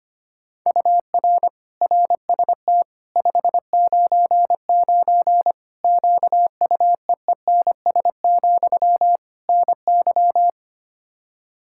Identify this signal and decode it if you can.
Morse code, heard as a keyed tone, about 25 wpm, 700 Hz. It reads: UR RST 599 QUEENS, NY
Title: Practice QSO - Exchange 1 - Beginning - 25wpm